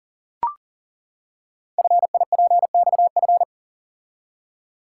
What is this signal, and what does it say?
Morse code, heard as a keyed tone, about 39 wpm, 690 Hz. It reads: FIPXF
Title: Code Groups - 5 Letters - 40wpm